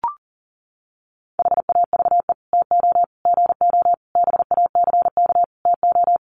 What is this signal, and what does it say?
Morse code, heard as a keyed tone, about 40 wpm, 700 Hz. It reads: HAVE TO GO BACK TO
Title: Sets of 5 Words Encore2 40wpm